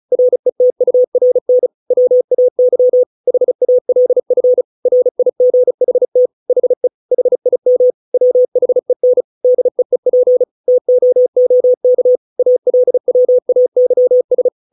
Morse code, heard as a keyed tone, about 35 wpm, 500 Hz. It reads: RETURN WAY HALF RIGHT HE HIM WHEN DEEP TOOK ALWAYS